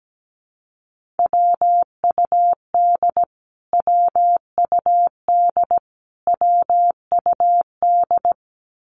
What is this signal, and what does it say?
Morse code, heard as a keyed tone, about 17 wpm, 700 Hz. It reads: WUD WUD WUD
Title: Abbreviations Repeated 3x - 17wpm